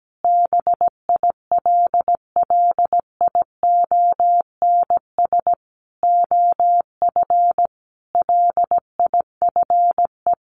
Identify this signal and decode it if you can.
Morse code, heard as a keyed tone, about 17 wpm, 700 Hz. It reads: BILLIONS OF LIFE
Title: A Sound Of Thunder 17wpm